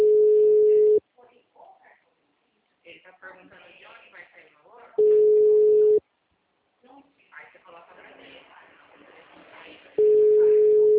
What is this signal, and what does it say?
Ringback tone (Europe, 425 Hz), with speech.